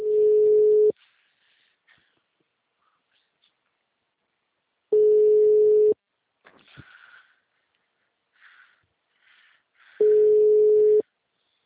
Ringback tone (Europe, 425 Hz).
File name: out- -2007- - - .1037